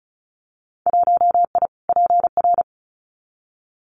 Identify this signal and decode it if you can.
Morse code, heard as a keyed tone, about 36 wpm, 700 Hz. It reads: 1I PR
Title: Winter Field Day Exchanges Encore 35wpm